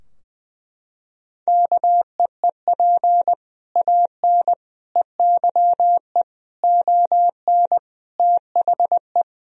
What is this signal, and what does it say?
Morse code, heard as a keyed tone, about 20 wpm, 700 Hz. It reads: KEEP AN EYE ON THE
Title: Sentences from Top 1000 Words - Review Part 3 - 20wpm